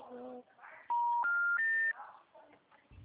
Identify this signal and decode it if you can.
Special information tone, with speech.